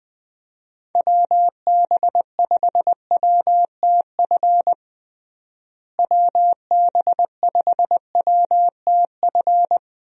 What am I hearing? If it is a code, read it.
Morse code, heard as a keyed tone, about 20 wpm, 700 Hz. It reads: WB5WTF WB5WTF
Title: USA Call Signs - 6 Characters - Repeated 2x - 20wpm